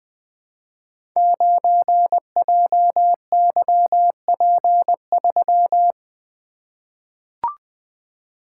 Morse code, heard as a keyed tone, about 20 wpm, 700 Hz. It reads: 9JYP3
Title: Code Groups - 5 Letter:Numbers - 20wpm